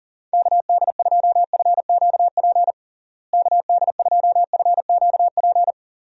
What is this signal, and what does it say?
Morse code, heard as a keyed tone, about 38 wpm, 700 Hz. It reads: KD2FQP KD2FQP
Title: USA Call Signs - 6 Characters - Repeated 3x - 40wpm